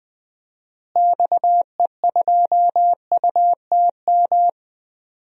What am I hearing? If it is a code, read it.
Morse code, heard as a keyed tone, about 20 wpm, 700 Hz. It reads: XE2UTM
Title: DXCC-100 Call Signs - 6 Characters - Repeated 2x - 20wpm